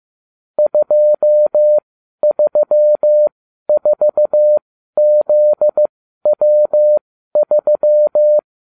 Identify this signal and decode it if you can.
Morse code, heard as a keyed tone, about 15 wpm, spaced as for 12 wpm, 600 Hz. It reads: 234ZW3